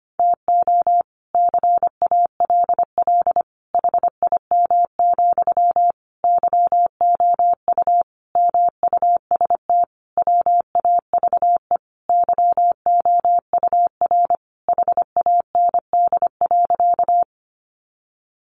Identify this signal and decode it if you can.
Morse code, heard as a keyed tone, about 25 wpm, 700 Hz. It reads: TO CALL HIM, YOU MUST WAVE YOUR HAND.
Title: Sentences from Top 1400 Words 25wpm